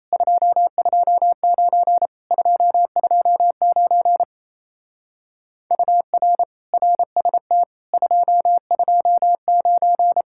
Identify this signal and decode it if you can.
Morse code, heard as a keyed tone, about 34 wpm, 700 Hz. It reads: 229 229 UR RST 229
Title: Speed-Racing Common QSO Phrases - 28wpm